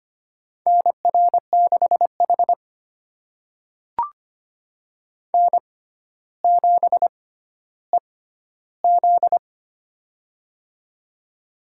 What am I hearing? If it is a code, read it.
Morse code, heard as a keyed tone, about 25 wpm, spaced as for 13 wpm, 700 Hz. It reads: NR6H N7EZ
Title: USA Call Signs - 4 Characters - 6x Character Spacing - 25wpm